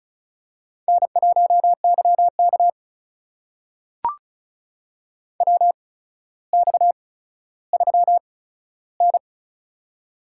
Morse code, heard as a keyed tone, about 36 wpm, spaced as for 15 wpm, 700 Hz. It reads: N1YK WX3N
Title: USA Call Signs - 4 Characters - 8x Character Spacing - 35wpm